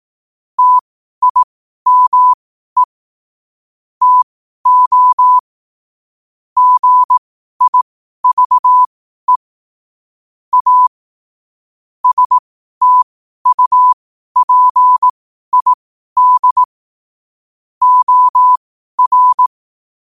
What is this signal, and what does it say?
Morse code, heard as a keyed tone, about 18 wpm, spaced as for 12 wpm, 1000 Hz. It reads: TIME TO GIVE A STUPID OR
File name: qotdfiles / 12wpm